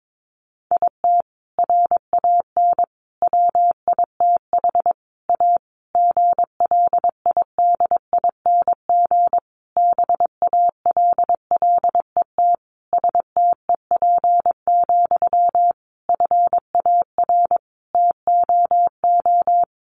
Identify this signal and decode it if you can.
Morse code, heard as a keyed tone, about 22 wpm, 700 Hz. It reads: IT RAN WITH A GLIDING BALLET STEP, FAR TOO
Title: A Sound Of Thunder 22wpm